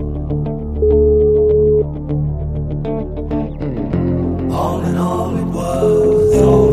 Ringback tone (Europe, 425 Hz), with noise.